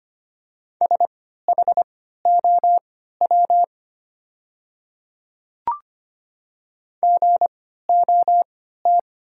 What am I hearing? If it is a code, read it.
Morse code, heard as a keyed tone, about 25 wpm, spaced as for 14 wpm, 690 Hz. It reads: SHOW GOT
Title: 3x Character Spacing - Top 500 Words - 25wpm